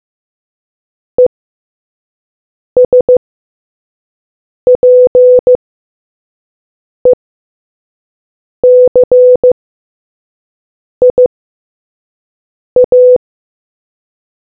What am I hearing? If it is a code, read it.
Morse code, heard as a keyed tone, about 15 wpm, spaced as for 5 wpm, 500 Hz. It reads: ESPECIA